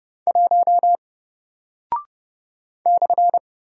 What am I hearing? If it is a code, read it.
Morse code, heard as a keyed tone, about 30 wpm, 700 Hz. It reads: 1 /
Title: Single Character (Letter:Number:Punctuation:Prosign) Alt 30wpm